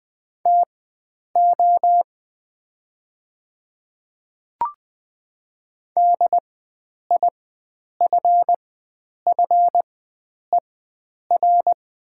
Morse code, heard as a keyed tone, about 20 wpm, spaced as for 9 wpm, 700 Hz. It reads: TO DIFFER
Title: 4x Character Spacing - Top 500 Words - 20wpm